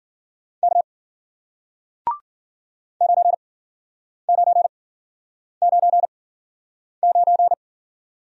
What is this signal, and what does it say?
Morse code, heard as a keyed tone, about 49 wpm, spaced as for 9 wpm, 700 Hz. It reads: K 9999
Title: Speed-Racing Single Letter-Number 40wpm